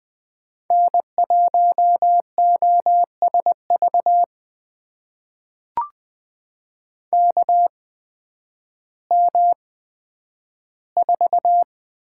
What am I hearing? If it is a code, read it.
Morse code, heard as a keyed tone, about 20 wpm, spaced as for 11 wpm, 700 Hz. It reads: N1OSV KM4
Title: USA Call Signs - 5 Characters - 8x Character Spacing - 20wpm